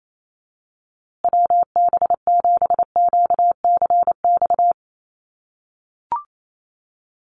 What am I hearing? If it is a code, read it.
Morse code, heard as a keyed tone, about 28 wpm, 700 Hz. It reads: WB7QCX
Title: USA Call Signs Repeated 2x 28wpm